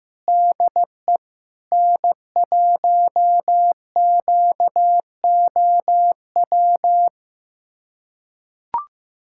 Morse code, heard as a keyed tone, about 15 wpm, 700 Hz. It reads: DE N1QOW
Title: USA Callsign DE Callsign 15wpm